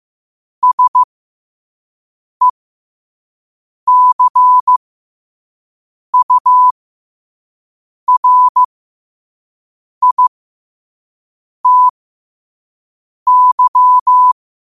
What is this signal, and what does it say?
Morse code, heard as a keyed tone, about 15 wpm, spaced as for 5 wpm, 1000 Hz. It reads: SECURITY